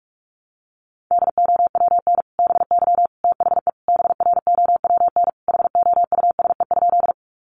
Morse code, heard as a keyed tone, about 44 wpm, 700 Hz. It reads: DOWN BY THE BROWN HOUSE?
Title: Sentences from Top 1200 Words 45wpm